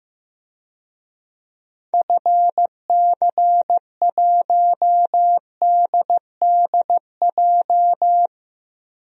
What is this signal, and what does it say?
Morse code, heard as a keyed tone, about 16 wpm, 700 Hz. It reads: FC1DDJ
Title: DXCC-100 Call Signs - 6 Characters - 15wpm